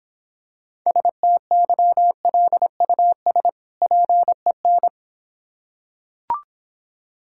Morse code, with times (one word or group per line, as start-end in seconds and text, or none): 0.9-3.5 STYLUS
3.8-4.9 PEN